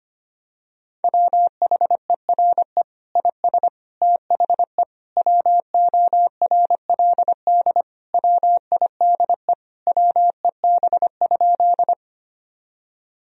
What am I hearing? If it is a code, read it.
Morse code, heard as a keyed tone, about 25 wpm, 700 Hz. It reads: WHERE IS THE WORLD WIDE WEB?
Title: Sentences from Top 1000 Words - Review Part 4 - 25wpm